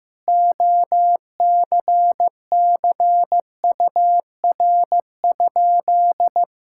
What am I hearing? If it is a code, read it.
Morse code, heard as a keyed tone, about 15 wpm, 700 Hz. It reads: OCCUR?
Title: Sentences from Top 1000 Words - Review Part 6 - 15wpm